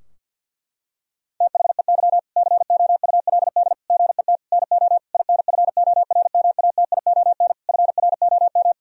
Morse code, heard as a keyed tone, about 50 wpm, 700 Hz. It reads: THEY COULD GET NO INFORMATION FROM
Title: Sentences from Top 1000 Words - Review Part 3 - 50wpm